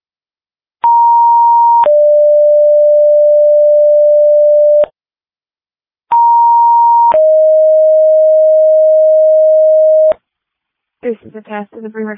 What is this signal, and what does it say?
Two-tone page, heard as a one-second tone, then a three-second tone, about 940 then 590 Hz, about 940 then 620 Hz, with speech. It is heard from 0.8 to 10.1 s.